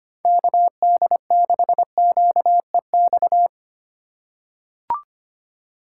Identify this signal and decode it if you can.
Morse code, heard as a keyed tone, about 25 wpm, 700 Hz. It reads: KD6QEX
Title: USA Call Signs - 6 Characters - 2x Character Spacing - 25wpm